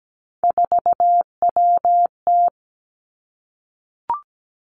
Morse code, heard as a keyed tone, about 17 wpm, 700 Hz. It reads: 4WT
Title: Three Challenging Letter-Number Combinations 17wpm